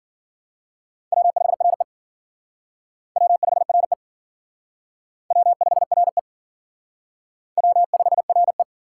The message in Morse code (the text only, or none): W5RE W5RE W5RE W5RE